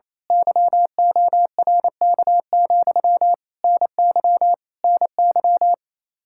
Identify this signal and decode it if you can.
Morse code, heard as a keyed tone, about 27 wpm, 700 Hz. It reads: YORK, NY NY
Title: Speed-Racing Common QSO Phrases - 28wpm